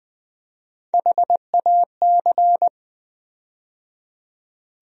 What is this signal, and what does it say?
Morse code, heard as a keyed tone, about 20 wpm, 700 Hz. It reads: HAC
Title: USA Call Sign Postfixes - 3 Letters 20wpm